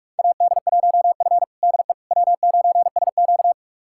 Morse code, heard as a keyed tone, about 46 wpm, 700 Hz. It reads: AD1F DE W9SQ
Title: USA Callsign DE Callsign - 4 Characters - 45wpm